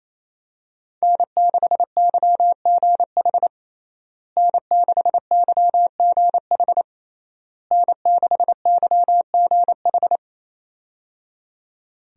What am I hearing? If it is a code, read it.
Morse code, heard as a keyed tone, about 28 wpm, 700 Hz. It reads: N6YGH N6YGH N6YGH
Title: USA Call Signs - 5 Characters - Repeated 3x - 28wpm